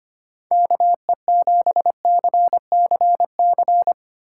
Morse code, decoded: KE7CCC

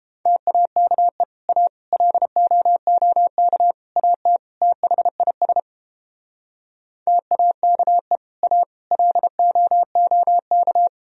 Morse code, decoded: TAKE A LOOK AT THIS TAKE A LOOK

